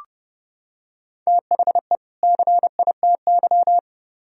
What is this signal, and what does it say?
Morse code, heard as a keyed tone, about 29 wpm, 700 Hz. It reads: THE CITY